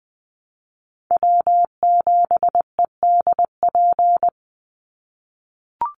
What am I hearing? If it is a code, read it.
Morse code, heard as a keyed tone, about 20 wpm, 700 Hz. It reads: W7EDP
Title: USA Call Signs 20wpm